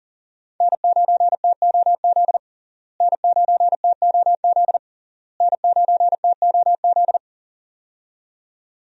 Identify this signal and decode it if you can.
Morse code, heard as a keyed tone, about 40 wpm, 700 Hz. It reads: N9TOZ N9TOZ N9TOZ
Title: USA Call Signs - 5 Characters - Repeated 3x - 40wpm